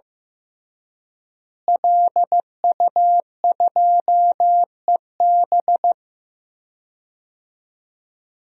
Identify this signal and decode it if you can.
Morse code, heard as a keyed tone, about 15 wpm, 700 Hz. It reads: LU2EB
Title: DXCC-100 Call Signs - 5 Characters - Repeated 3x - 15wpm